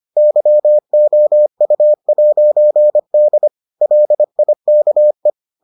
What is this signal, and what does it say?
Morse code, heard as a keyed tone, about 25 wpm, 600 Hz. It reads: YOU'D LIKE